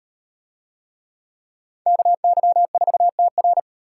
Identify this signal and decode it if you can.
Morse code, heard as a keyed tone, about 37 wpm, 700 Hz. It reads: KY4TR